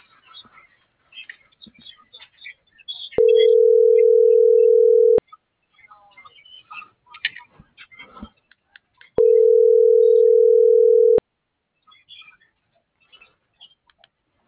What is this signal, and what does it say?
Ringback tone (North America), 440 + 480 Hz.